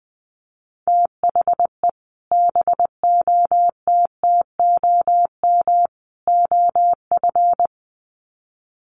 Morse code, heard as a keyed tone, about 20 wpm, 700 Hz. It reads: THE BOTTOM OF
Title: Sets of 3 Words 20wpm